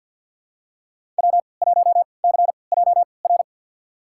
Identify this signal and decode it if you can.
Morse code, heard as a keyed tone, about 51 wpm, spaced as for 29 wpm, 700 Hz. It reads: W1CJR
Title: USA Call Signs - 3x Character Spacing - 50wpm